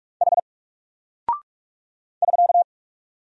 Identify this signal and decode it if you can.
Morse code, heard as a keyed tone, about 48 wpm, 700 Hz. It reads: H <SK>
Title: Single Character (Letter:Number:Punctuation:Prosign) Alt 45wpm